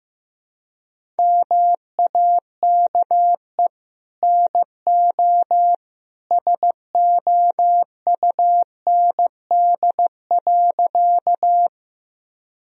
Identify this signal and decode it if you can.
Morse code, heard as a keyed tone, about 16 wpm, 700 Hz. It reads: MAKE NO SOUND.
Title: Sentences from Top 1000 Words - Review Part 1 - 15wpm